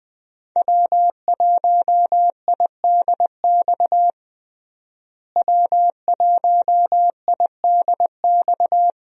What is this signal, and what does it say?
Morse code, heard as a keyed tone, about 20 wpm, 700 Hz. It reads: W1IDX W1IDX
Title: USA Call Signs - 5 Characters - Repeated 2x - 20wpm